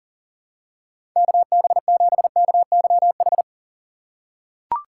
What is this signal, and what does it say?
Morse code, heard as a keyed tone, about 40 wpm, 700 Hz. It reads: KB7KYH
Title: USA Call Signs - 6 Characters - 4x Character Spacing - 40wpm